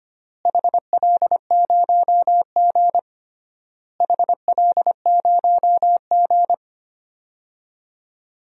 Morse code, heard as a keyed tone, about 25 wpm, 700 Hz. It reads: HL0G HL0G
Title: DXCC-100 Call Signs - 4 Characters - Repeated 3x - 25wpm